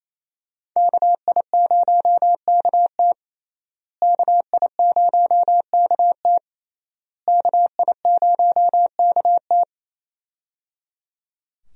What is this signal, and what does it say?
Morse code, heard as a keyed tone, about 28 wpm, 700 Hz. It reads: KI0KT KI0KT KI0KT
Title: USA Call Signs - 5 Characters - Repeated 3x - 28wpm